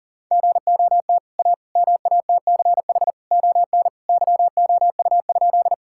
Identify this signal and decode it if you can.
Morse code, heard as a keyed tone, about 40 wpm, 700 Hz. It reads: GOT A MATCH ON YOU?
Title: Sentences from Top 1600 Words 40wpm